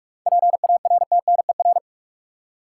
Morse code, heard as a keyed tone, about 45 wpm, 700 Hz. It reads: PARTNER
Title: Seven Letter Words - 45wpm